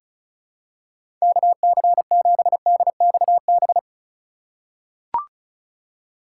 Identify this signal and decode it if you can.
Morse code, heard as a keyed tone, about 34 wpm, 700 Hz. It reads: KC7DXB
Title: USA Call Signs - 6 Characters - 3x Character Spacing - 35wpm